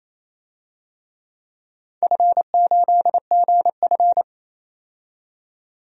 Morse code, heard as a keyed tone, about 27 wpm, 700 Hz. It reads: F8GF
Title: DXCC-100 Call Signs - 4 Characters - 28wpm